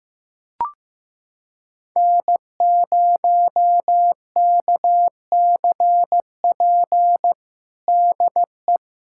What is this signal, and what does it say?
Morse code, heard as a keyed tone, about 15 wpm, 700 Hz. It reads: N0KCP DE
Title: USA Callsign DE Callsign 15wpm